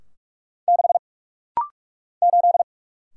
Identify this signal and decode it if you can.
Morse code, heard as a keyed tone, about 46 wpm, 700 Hz. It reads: 6 8
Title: Single Letter-Number - Rapid-Fire 45wpm